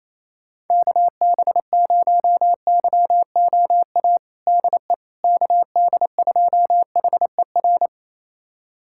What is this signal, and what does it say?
Morse code, heard as a keyed tone, about 28 wpm, 700 Hz. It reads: KB0YOA DE KD2HER
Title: USA Callsign DE Callsign - 6 Characters - 28wpm